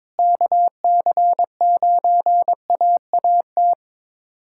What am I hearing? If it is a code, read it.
Morse code, heard as a keyed tone, about 22 wpm, 700 Hz. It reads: KC9AAT
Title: USA Callsign DE Callsign - 6 Characters - 22wpm